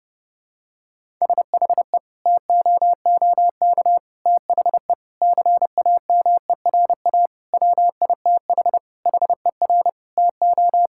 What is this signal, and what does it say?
Morse code, heard as a keyed tone, about 30 wpm, 700 Hz. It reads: SHE TOOK THE CAMERA WITH HER TO
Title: Sentences from Top 1000 Words - Review Part 5 - 30wpm